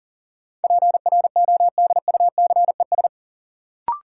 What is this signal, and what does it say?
Morse code, heard as a keyed tone, about 37 wpm, 700 Hz. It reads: PRODUCES
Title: Sentences from Top 1000 Words - Review Part 6 - New Word Drill - 40wpm